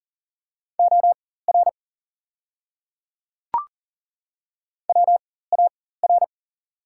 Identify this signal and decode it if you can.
Morse code, heard as a keyed tone, about 40 wpm, spaced as for 19 wpm, 690 Hz. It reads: OR WAR
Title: 4x Character Spacing - Top 500 Words - 40wpm